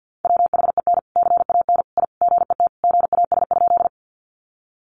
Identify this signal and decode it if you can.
Morse code, heard as a keyed tone, about 50 wpm, 700 Hz. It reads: WHEN CAN I GET GAS?
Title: Sentences from Top 1100 Words 50wpm